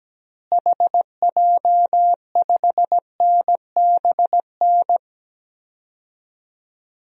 Morse code, heard as a keyed tone, about 17 wpm, 700 Hz. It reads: HJ5NBN
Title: DXCC-100 Call Signs - 6 Characters - Repeated 2x - 17wpm